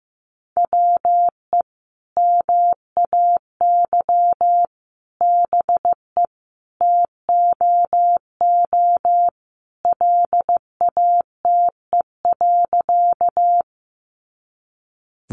Morse code, heard as a keyed tone, about 15 wpm, 700 Hz. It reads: WE MAY BE TOO LATE.
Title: Sentences from Top 300 Words 15wpm